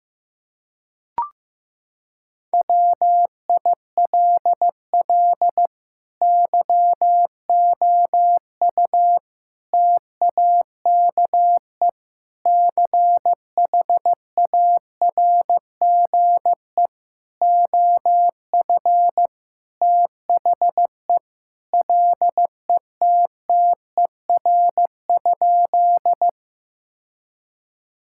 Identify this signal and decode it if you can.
Morse code, heard as a keyed tone, about 15 wpm, 700 Hz. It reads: WILL YOU TAKE CHARGE OF THE LETTER?
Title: Sentences from Top 1000 Words - Review Part 6 - 15wpm